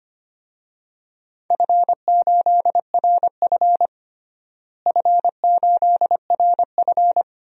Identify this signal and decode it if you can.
Morse code, heard as a keyed tone, about 25 wpm, 700 Hz. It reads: F8RF F8RF
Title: DXCC-100 Call Signs - 4 Characters - Repeated 3x - 25wpm